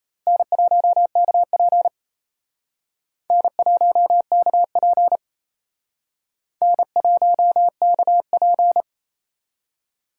Morse code, heard as a keyed tone, about 32 wpm, 700 Hz. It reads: N1KP N1KP N1KP
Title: Speed Racing - USA 1x2 Call Signs 28wpm